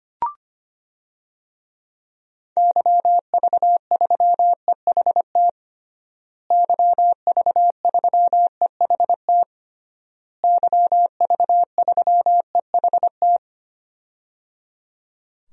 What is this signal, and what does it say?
Morse code, heard as a keyed tone, about 25 wpm, 700 Hz. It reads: YV3EHT YV3EHT YV3EHT
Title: DXCC-100 Call Signs - 6 Characters - Repeated 3x - 25wpm